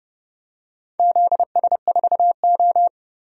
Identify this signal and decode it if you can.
Morse code, heard as a keyed tone, about 30 wpm, 700 Hz. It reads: ZS4O
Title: DXCC-100 Call Signs - 4 Characters - Repeated 3x - 30wpm